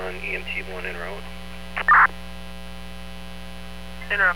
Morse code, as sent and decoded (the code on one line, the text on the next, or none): none